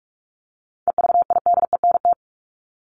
Morse code, heard as a keyed tone, about 44 wpm, 700 Hz. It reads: EVIDENT
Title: LY Suffix - Top 100 Words with Base Words 45wpm